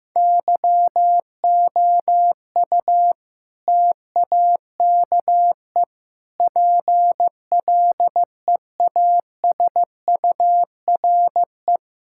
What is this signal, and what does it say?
Morse code, heard as a keyed tone, about 15 wpm, 700 Hz. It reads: YOU TAKE PLEASURE
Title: Sentences from Top 1600 Words 15wpm